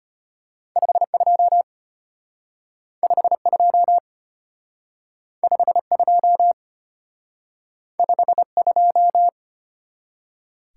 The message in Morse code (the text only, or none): 52 52 52 52